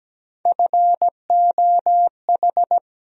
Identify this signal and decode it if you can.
Morse code, heard as a keyed tone, about 17 wpm, 700 Hz. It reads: FOH
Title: USA Call Sign Postfixes - 3 Letters 17wpm